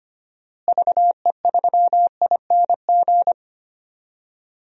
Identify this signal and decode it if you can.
Morse code, heard as a keyed tone, about 25 wpm, 700 Hz. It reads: VE3ING
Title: DXCC-100 Call Signs - 6 Characters - Repeated 3x - 25wpm